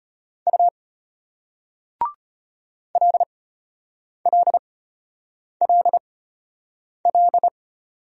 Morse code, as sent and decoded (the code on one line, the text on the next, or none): ..- / .-.. .-.. .-.. .-..
U LLLL